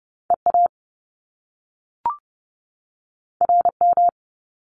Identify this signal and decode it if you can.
Morse code, heard as a keyed tone, about 30 wpm, 700 Hz. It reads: EA RM
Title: Top Two Letter Combinations in Words 30wpm